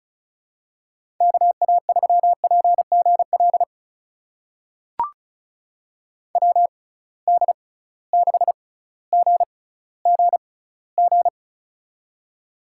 Morse code, heard as a keyed tone, about 36 wpm, spaced as for 18 wpm, 700 Hz. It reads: KA3PGL WD6GGG
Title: USA Call Signs - 6 Characters - 6x Character Spacing - 35wpm